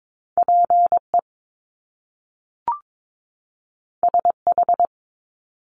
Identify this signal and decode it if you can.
Morse code, heard as a keyed tone, about 22 wpm, 700 Hz. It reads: PE SH
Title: Top Two Letter Combinations in Words 22wpm